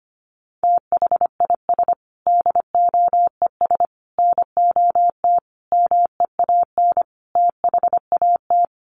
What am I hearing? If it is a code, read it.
Morse code, heard as a keyed tone, about 25 wpm, 700 Hz. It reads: THIS DOES NOT MEAN THAT
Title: Sets of 5 Words Encore2 25wpm